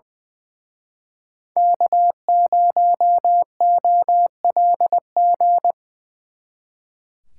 Morse code, heard as a keyed tone, about 23 wpm, 700 Hz. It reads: K0OLG
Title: Speed Racing - USA 1x3 Call Signs 20wpm